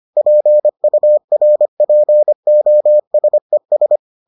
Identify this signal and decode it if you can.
Morse code, heard as a keyed tone, about 25 wpm, 600 Hz. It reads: PURPOSES